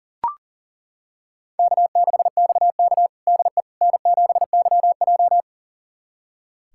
Morse code, heard as a keyed tone, about 40 wpm, 700 Hz. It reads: K6XK DE N7YJ
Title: USA Callsign DE Callsign - 4 Characters - 40wpm